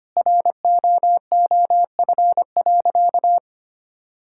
Morse code, heard as a keyed tone, about 25 wpm, 700 Hz. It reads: ROOF.
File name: practice / 25wpm / Sentences with 3-Word Phrases - 25wpm